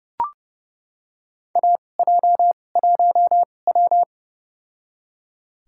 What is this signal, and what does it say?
Morse code, heard as a keyed tone, about 30 wpm, spaced as for 21 wpm, 700 Hz. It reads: AJ1W
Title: USA Call Signs - 4 Characters - 2x Character Spacing - 30wpm